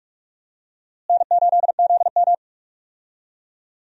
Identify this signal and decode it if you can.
Morse code, heard as a keyed tone, about 46 wpm, 700 Hz. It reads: N8ZM
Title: USA Call Signs - 4 Characters - Repeated 3x - 45wpm